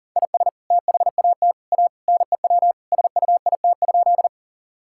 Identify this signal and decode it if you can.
Morse code, heard as a keyed tone, about 40 wpm, 700 Hz. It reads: IS THAT A NEW SUIT?
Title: Sentences from Top 1600 Words 40wpm